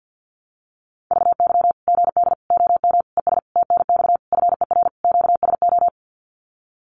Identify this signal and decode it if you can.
Morse code, heard as a keyed tone, about 49 wpm, 700 Hz. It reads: VY GD OM ES TNX FER QSO
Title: Common QSO Phrases 50wpm